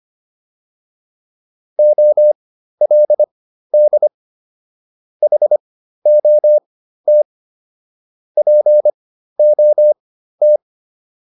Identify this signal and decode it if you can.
Morse code, heard as a keyed tone, about 25 wpm, spaced as for 13 wpm, 600 Hz. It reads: OLD HOT POT